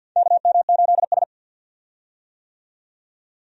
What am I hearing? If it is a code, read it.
Morse code, heard as a keyed tone, about 47 wpm, 700 Hz. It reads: KM8S
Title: USA Call Signs - 4 Characters - 50wpm